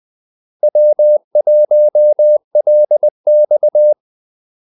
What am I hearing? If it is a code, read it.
Morse code, heard as a keyed tone, about 20 wpm, 600 Hz. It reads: W1LX
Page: Kaldesignaler 20 wpm